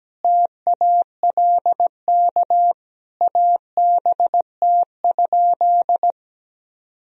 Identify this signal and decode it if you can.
Morse code, heard as a keyed tone, about 17 wpm, 700 Hz. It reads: TALK ABT?
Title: Sentences from Top 400 Words and Abbreviations 17wpm